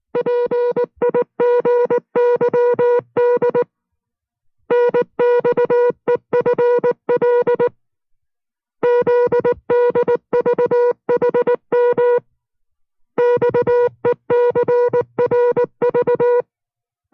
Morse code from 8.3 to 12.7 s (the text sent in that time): ZDVHM